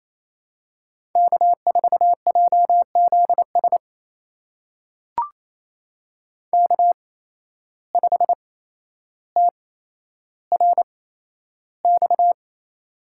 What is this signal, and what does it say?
Morse code, heard as a keyed tone, about 27 wpm, spaced as for 12 wpm, 700 Hz. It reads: K4JZS K5TRX
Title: USA Call Signs - 5 Characters - 8x Character Spacing - 28wpm